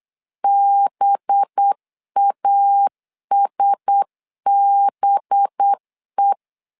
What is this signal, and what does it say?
Morse code, heard as a keyed tone, about 8 wpm, 800 Hz. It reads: BASBE
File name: LEZIONI / 5 LBVC / 5 LB 8wpm 2816